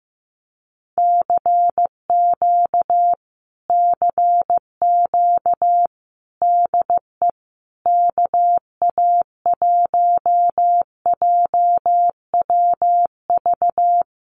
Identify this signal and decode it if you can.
Morse code, heard as a keyed tone, about 15 wpm, 700 Hz. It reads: CQ CQ DE KA1JWV
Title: Common QSO Phrases Encore 15wpm